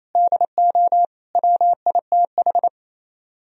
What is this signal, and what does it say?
Morse code, heard as a keyed tone, about 34 wpm, 700 Hz. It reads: DO WITH
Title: Speed-Racing - Sets of 4 Words 28wpm